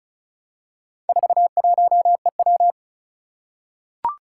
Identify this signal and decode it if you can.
Morse code, heard as a keyed tone, about 36 wpm, 700 Hz. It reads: 41EW
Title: Four Challenging Letter-Number Combinations 35wpm